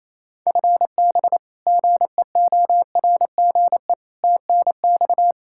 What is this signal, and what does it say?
Morse code, heard as a keyed tone, about 27 wpm, 700 Hz. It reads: FB GEORGE TNX
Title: K1USN SST QSO Practice 28wpm